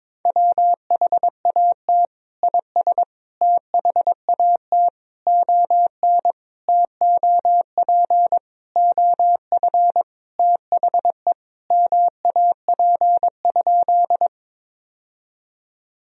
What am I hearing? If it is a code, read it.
Morse code, heard as a keyed tone, about 21 wpm, 700 Hz. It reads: WHAT IS THAT ON TOP OF THE MAP?
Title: Sentences from Top 1000 Words - Review Part 2 - 22wpm